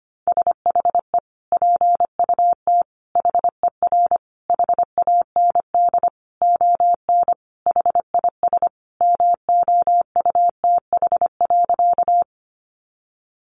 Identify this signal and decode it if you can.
Morse code, heard as a keyed tone, about 25 wpm, 700 Hz. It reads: SHE PUT HER HAND ON HIS MOUTH.
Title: Sentences from Top 1500 Words 25wpm